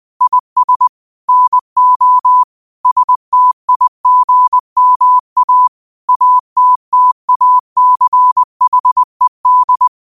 Morse code, heard as a keyed tone, about 20 wpm, 1000 Hz. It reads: IS NO STIGMA ATTACHED
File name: qotdfiles / 20wpm